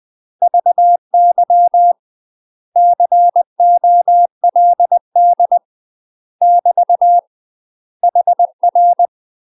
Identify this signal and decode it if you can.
Morse code, heard as a keyed tone, about 20 wpm, 700 Hz. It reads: VY COLD <BT> HR
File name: UK MASTER QSOs TRAINING/20WPM